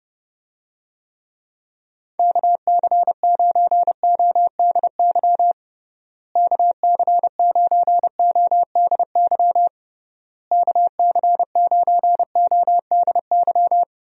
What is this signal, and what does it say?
Morse code, heard as a keyed tone, about 32 wpm, 700 Hz. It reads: KC9ODY KC9ODY KC9ODY
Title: USA Call Signs - 6 Characters - Repeated 3x - 30wpm